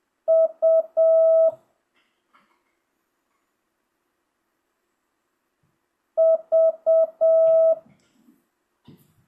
Morse code, heard as a keyed tone, about 7 wpm, 650 Hz. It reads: U V